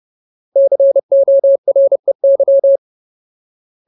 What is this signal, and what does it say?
Morse code, heard as a keyed tone, about 30 wpm, 550 Hz. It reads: COREY
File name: Ultimate Word List / 30WPM Ultimate